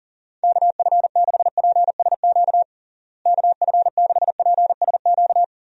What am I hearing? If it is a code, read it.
Morse code, heard as a keyed tone, about 40 wpm, 700 Hz. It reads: KF6PSQ KF6PSQ
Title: USA Call Signs - 6 Characters - Repeated 2x - 40wpm